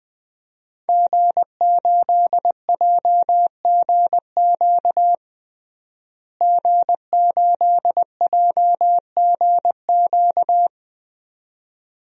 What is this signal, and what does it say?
Morse code, heard as a keyed tone, about 20 wpm, 700 Hz. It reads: G8JGQ G8JGQ
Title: DXCC-100 Call Signs - 5 Characters - Repeated 2x - 20wpm